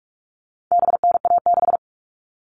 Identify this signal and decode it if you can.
Morse code, heard as a keyed tone, about 44 wpm, 700 Hz. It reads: BNA6
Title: All Cut Numbers - 4-Digit 45wpm